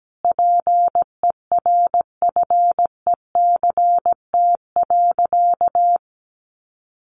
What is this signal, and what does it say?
Morse code, heard as a keyed tone, about 17 wpm, 700 Hz. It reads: PERFECT.
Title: Sentences from Top 1300 Words 17wpm